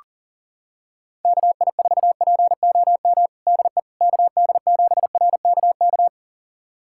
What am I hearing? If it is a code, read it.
Morse code, heard as a keyed tone, about 40 wpm, 700 Hz. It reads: KI4POM DE KD7RKK
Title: USA Callsign DE Callsign - 6 Characters - 40wpm